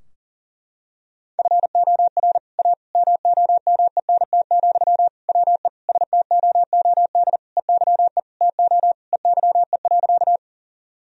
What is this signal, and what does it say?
Morse code, heard as a keyed tone, about 40 wpm, 700 Hz. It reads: FOR A MOMENT, WE STOOD EYE TO EYE.
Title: Sentences from Top 1000 Words - Review Part 2 - 40wpm